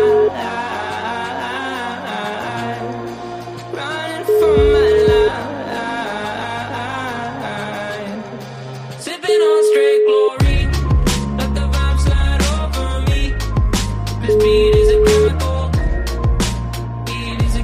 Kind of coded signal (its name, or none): telephone call progress tone